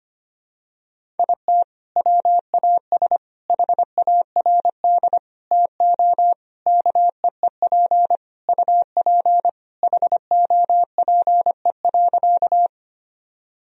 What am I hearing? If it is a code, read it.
Morse code, heard as a keyed tone, about 25 wpm, 700 Hz. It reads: IT WAS HARD TO KEEP UP HOPE.
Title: Sentences from Top 1000 Words - Review Part 1 - 25wpm